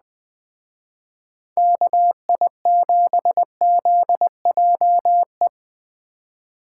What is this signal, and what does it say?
Morse code, heard as a keyed tone, about 20 wpm, 700 Hz. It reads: KI7ZJE
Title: Speed Racing - USA 2x3 Call Signs 17wpm